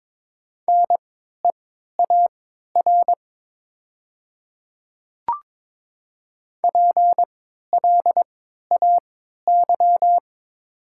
Morse code, heard as a keyed tone, about 22 wpm, spaced as for 12 wpm, 700 Hz. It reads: NEAR PLAY
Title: 3x Character Spacing - Top 200 Words - 22wpm